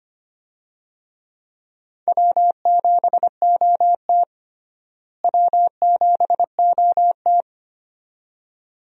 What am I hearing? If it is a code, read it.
Morse code, heard as a keyed tone, about 25 wpm, 700 Hz. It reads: W7OT W7OT
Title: USA Call Signs - 4 Characters - Repeated 2x - 25wpm